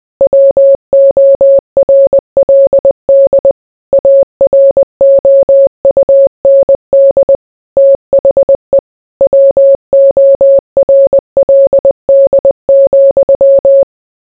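Morse code, heard as a keyed tone, about 20 wpm, 550 Hz. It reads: WORLD AROUND THE WORLD,
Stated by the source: Morse code mixed with noise and music as background sound for sleep.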